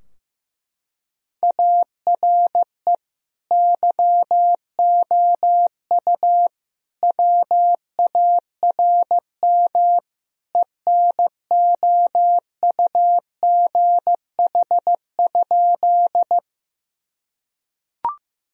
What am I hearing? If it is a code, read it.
Morse code, heard as a keyed tone, about 15 wpm, 700 Hz. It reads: ARE YOU WARM ENOUGH?
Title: Sentences from Top 1000 Words - Review Part 2 - 15wpm